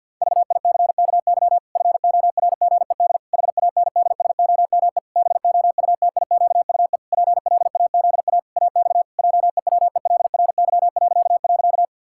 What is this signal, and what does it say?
Morse code, heard as a keyed tone, about 52 wpm, 700 Hz. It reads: VICKY FORGED HANDSOME BOUTIQUE PLAZA AX JEWELRY..
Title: Pangram Challenge v2 - 50wpm